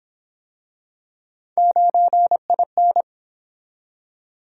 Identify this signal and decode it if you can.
Morse code, heard as a keyed tone, about 29 wpm, 700 Hz. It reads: 9IN